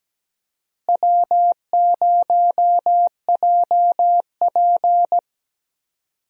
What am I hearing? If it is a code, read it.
Morse code, heard as a keyed tone, about 17 wpm, 700 Hz. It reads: W0JP
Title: USA Call Signs - 4 Characters - Repeated 2x - 17wpm